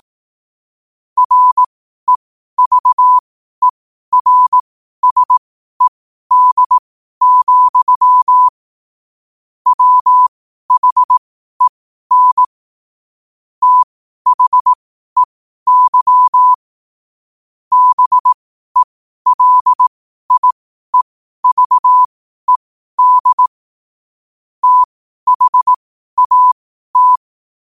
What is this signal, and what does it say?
Morse code, heard as a keyed tone, about 18 wpm, spaced as for 12 wpm, 1000 Hz. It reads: REVERSED, WHEN THEY BELIEVED THAT
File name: qotdfiles / 12wpm